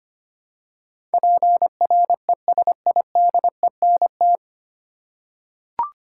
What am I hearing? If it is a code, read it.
Morse code, heard as a keyed tone, about 25 wpm, 700 Hz. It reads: PRESIDENT
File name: practice / 25wpm / 9 Letter Words - First 3 Letters Given - 25wpm